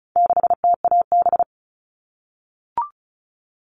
Morse code, heard as a keyed tone, about 34 wpm, 700 Hz. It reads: 6TAB
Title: All Cut Numbers - 4-Digit 35wpm